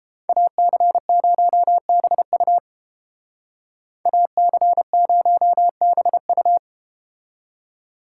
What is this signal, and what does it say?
Morse code, heard as a keyed tone, about 32 wpm, 700 Hz. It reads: AC0BU AC0BU